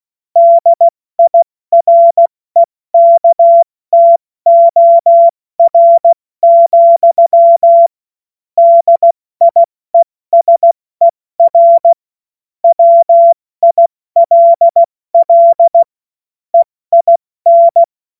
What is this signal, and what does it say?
Morse code, heard as a keyed tone, about 16 wpm, spaced as for 14 wpm, 680 Hz. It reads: DIREKTOR, DIESER WILL EIN
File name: Morsekurs / Witze / Witz 14. September 2021, 70 F80